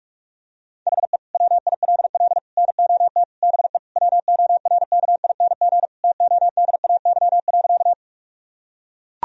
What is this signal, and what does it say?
Morse code, heard as a keyed tone, about 41 wpm, 700 Hz. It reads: HE WILL NOT BE WORKING TODAY.
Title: Sentences from Top 1000 Words - Review Part 4 - 45wpm